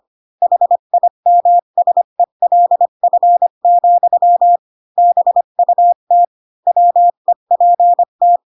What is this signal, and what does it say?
Morse code, heard as a keyed tone, about 25 wpm, 700 Hz. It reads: HIMSELF, BUT WEPT
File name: Pinocchio / Pinocchio ch3 20 25 ik1hge